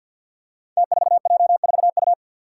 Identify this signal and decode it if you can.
Morse code, heard as a keyed tone, about 51 wpm, 700 Hz. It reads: T4J4U
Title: Five Challenging Letter-Number Combinations 50wpm